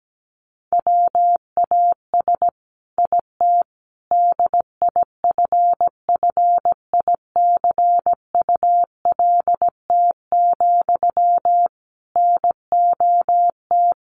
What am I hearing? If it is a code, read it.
Morse code, heard as a keyed tone, about 17 wpm, 700 Hz. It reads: WAS IT DIFFICULT, NOT
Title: Sentences from Top 1200 Words 17wpm